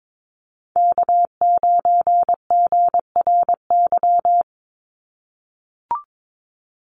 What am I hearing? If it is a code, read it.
Morse code, heard as a keyed tone, about 22 wpm, 700 Hz. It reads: K9GRY
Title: USA Call Signs 22wpm